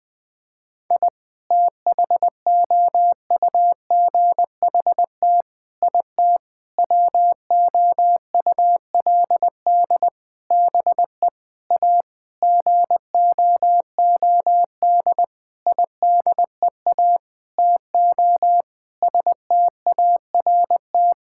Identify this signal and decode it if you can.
Morse code, heard as a keyed tone, about 19 wpm, 700 Hz. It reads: I THOUGHT IT WOULD BE A GOOD IDEA TO START
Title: Sentences with 9-Word Phrases - Spoken Beginning - 20wpm